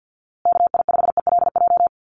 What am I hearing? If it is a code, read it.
Morse code, heard as a keyed tone, about 48 wpm, 700 Hz. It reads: KI5ELJ
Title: USA Call Signs Repeated 2x 50wpm